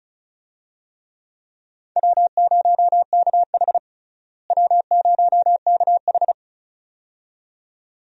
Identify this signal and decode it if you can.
Morse code, heard as a keyed tone, about 36 wpm, 700 Hz. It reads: W0KH W0KH
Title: USA Call Signs - 4 Characters - Repeated 2x - 35wpm